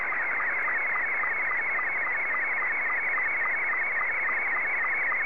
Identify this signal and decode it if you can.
SSTV (slow-scan television), Scottie 2.
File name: sstv / S2-JA3OE-ILF